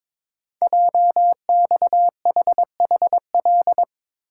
Morse code, heard as a keyed tone, about 22 wpm, 700 Hz. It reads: JXHHL
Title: Code Groups - 5 Letters - 22wpm